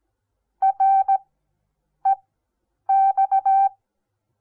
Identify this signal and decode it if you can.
Morse code, heard as a keyed tone, about 17 wpm, spaced as for 8 wpm, 780 Hz. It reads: REX
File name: Onlinelehrgang / morsen / DJ4UF Morsen08